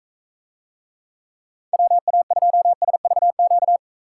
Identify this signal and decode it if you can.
Morse code, heard as a keyed tone, about 43 wpm, 700 Hz. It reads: WA2SVQ